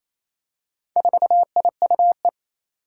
Morse code, heard as a keyed tone, about 27 wpm, 700 Hz. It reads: 4IUE
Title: Four Challenging Letter-Number Combinations 28wpm